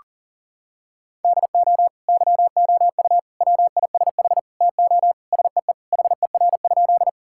Morse code, heard as a keyed tone, about 40 wpm, 700 Hz. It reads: DO YOU WISH TO SEE HER?
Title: Sentences from Top 1600 Words 40wpm